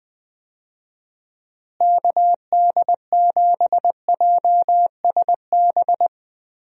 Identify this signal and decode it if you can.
Morse code, heard as a keyed tone, about 20 wpm, 700 Hz. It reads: KD7JSB
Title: USA Call Signs - 6 Characters - Repeated 3x - 20wpm